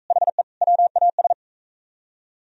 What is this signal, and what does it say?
Morse code, heard as a keyed tone, about 42 wpm, 700 Hz. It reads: HE WAS